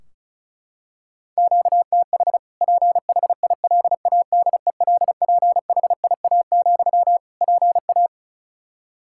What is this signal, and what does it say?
Morse code, heard as a keyed tone, about 36 wpm, 700 Hz. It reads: QTH PHILADELPHIA, PA
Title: Practice QSO - Exchange 1 - Beginning - 35wpm